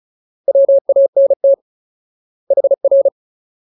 Morse code, heard as a keyed tone, about 34 wpm, 550 Hz. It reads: WANT HR
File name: Ultimate Word List / 35WPM Ultimate